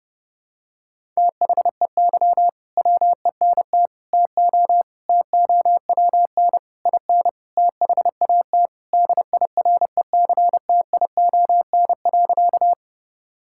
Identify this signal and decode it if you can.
Morse code, heard as a keyed tone, about 29 wpm, 700 Hz. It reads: THEY WENT TO TOWN IN THAT DIRECTION.
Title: Sentences from Top 1000 Words - Review Part 5 - 30wpm